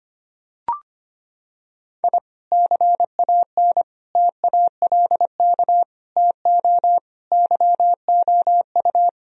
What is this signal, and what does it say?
Morse code, heard as a keyed tone, about 25 wpm, 700 Hz. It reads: I CAN TALK TO YOU
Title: Sentences from Top 1000 Words - Review Part 4 - 25wpm